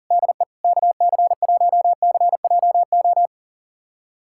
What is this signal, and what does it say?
Morse code, heard as a keyed tone, about 37 wpm, 700 Hz. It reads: DE KC1CJO
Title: USA Callsign DE Callsign 40wpm